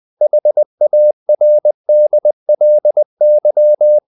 Morse code, heard as a keyed tone, about 20 wpm, 600 Hz. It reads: HARDLY
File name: the old man and the sea / the old man and the sea part 1 20wpm